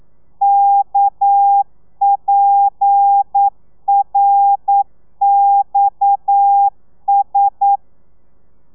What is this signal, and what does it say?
Morse code, heard as a keyed tone, about 9 wpm, 800 Hz. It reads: KPRXS
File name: telegrafia / i2tfj / Lez7 40